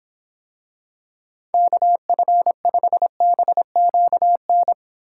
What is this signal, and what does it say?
Morse code, heard as a keyed tone, about 24 wpm, 700 Hz. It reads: KF5BQN